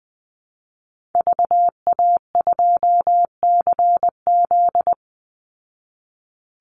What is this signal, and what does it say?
Morse code, heard as a keyed tone, about 20 wpm, 700 Hz. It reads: VA2CZ
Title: Call Signs Repeated 3x - 20wpm